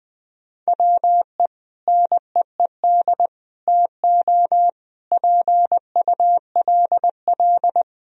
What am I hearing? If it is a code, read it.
Morse code, heard as a keyed tone, about 20 wpm, 700 Hz. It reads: WE NEED TO PULL
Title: Sentences from Top 1000 Words - Review Part 2 - 20wpm